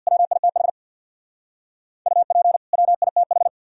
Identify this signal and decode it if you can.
Morse code, heard as a keyed tone, about 66 wpm, 700 Hz. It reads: WITH UP WITH